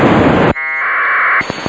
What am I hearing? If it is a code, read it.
Packet radio (AX.25, as in APRS).